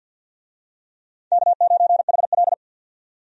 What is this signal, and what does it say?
Morse code, heard as a keyed tone, about 48 wpm, 700 Hz. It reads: K9HL
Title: USA Call Signs - 4 Characters - 50wpm